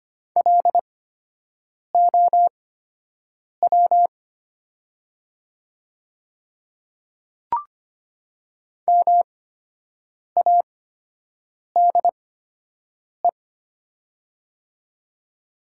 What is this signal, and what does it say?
Morse code, heard as a keyed tone, about 25 wpm, spaced as for 7 wpm, 700 Hz. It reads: LOW MADE
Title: 8x Character Spacing - Top 300 Words - 25wpm